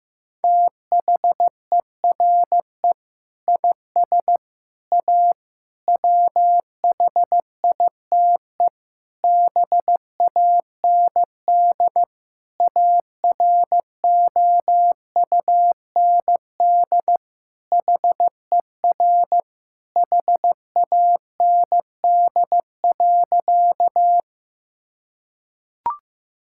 Morse code, heard as a keyed tone, about 15 wpm, 700 Hz. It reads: THERE IS A WHITE BAND AROUND HER HAND.
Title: Sentences from Top 1600 Words 15wpm